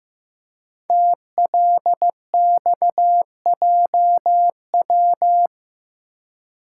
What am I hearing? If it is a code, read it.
Morse code, heard as a keyed tone, about 15 wpm, 700 Hz. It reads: TLXJW
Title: Code Groups - 5 Letters - 15wpm